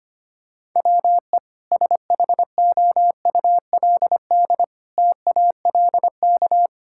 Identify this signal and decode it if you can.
Morse code, heard as a keyed tone, about 25 wpm, 700 Hz. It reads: WE SHOULD TALK
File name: practice / 25wpm / Sentences with 2-Word Phrases - 25wpm